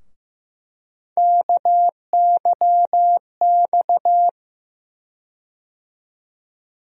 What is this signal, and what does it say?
Morse code, heard as a keyed tone, about 15 wpm, 700 Hz. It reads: KYX
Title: USA Call Sign Postfixes - 3 Letters 15wpm